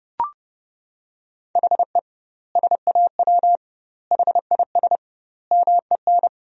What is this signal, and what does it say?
Morse code, heard as a keyed tone, about 32 wpm, 700 Hz. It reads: HE SAW HIS MEN
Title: 2x Word-Spacing - Sentences from Top 300 Words - 30wpm